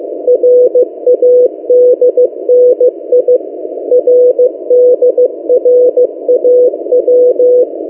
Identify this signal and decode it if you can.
Morse code, heard as a keyed tone, about 15 wpm, 500 Hz. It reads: RADNI RDRAW
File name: M08a- -1355utc-8096khz